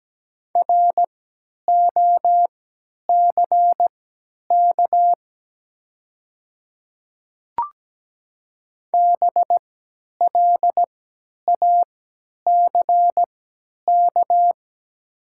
Morse code, heard as a keyed tone, about 17 wpm, spaced as for 10 wpm, 700 Hz. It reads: ROCK BLACK
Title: 3x Character Spacing - Top 500 Words - 17wpm